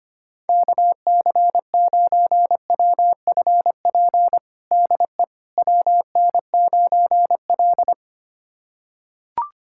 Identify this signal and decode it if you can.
Morse code, heard as a keyed tone, about 25 wpm, 700 Hz. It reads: KC9WFP DE WN9L
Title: USA Callsign DE Callsign 25wpm